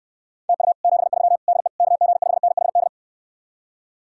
Morse code, heard as a keyed tone, about 69 wpm, 700 Hz. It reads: TU 73 DE KG5NHD